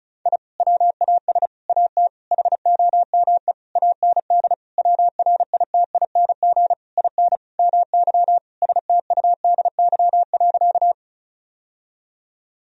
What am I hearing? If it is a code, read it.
Morse code, heard as a keyed tone, about 36 wpm, 700 Hz. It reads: I WAS AT HOME AND WRITING IN MY STUDY.
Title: Sentences from Top 1000 Words - Review Part 1 - 35wpm